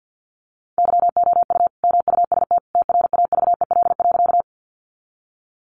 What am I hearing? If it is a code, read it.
Morse code, heard as a keyed tone, about 50 wpm, 700 Hz. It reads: YOU MUST TRAVEL.
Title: Sentences from Top 400 Words 50wpm